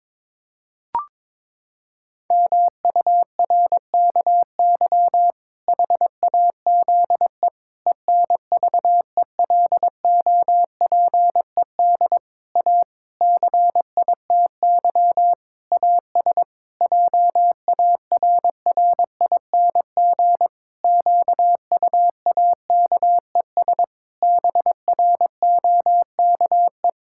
Morse code, heard as a keyed tone, about 22 wpm, 700 Hz. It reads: MURKY HAZE ENVELOPED A CITY AS JARRING QUAKES BROKE
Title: Pangram Challenge v2 - 22wpm